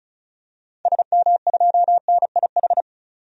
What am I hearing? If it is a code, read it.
Morse code, heard as a keyed tone, about 34 wpm, 700 Hz. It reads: SM2NIH
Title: DXCC-100 Call Signs - 6 Characters - 35wpm